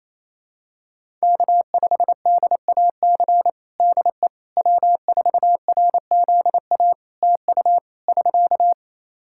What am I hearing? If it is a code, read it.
Morse code, heard as a keyed tone, about 28 wpm, 700 Hz. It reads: K5DAC DE W4RZA TU <SK>
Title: Practice QSO - Exchange 4 - Ending - 28wpm